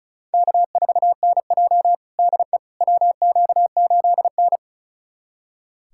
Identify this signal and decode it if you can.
Morse code, heard as a keyed tone, about 36 wpm, 700 Hz. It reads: K4NJ DE WQ8N
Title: USA Callsign DE Callsign - 4 Characters - 35wpm